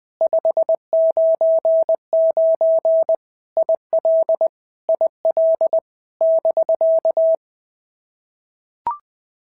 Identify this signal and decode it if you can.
Morse code, heard as a keyed tone, about 20 wpm, 650 Hz. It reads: 599 IL IL <BK>
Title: POTA QSO Practice - 20wpm